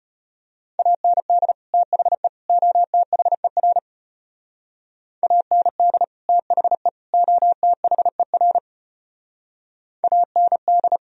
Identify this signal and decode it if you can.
Morse code, heard as a keyed tone, about 33 wpm, 700 Hz. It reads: AND THE OTHER AND THE OTHER AND